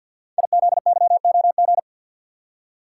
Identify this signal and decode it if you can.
Morse code, heard as a keyed tone, about 50 wpm, 690 Hz. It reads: IZYOG
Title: Code Groups - 5 Letters - 50wpm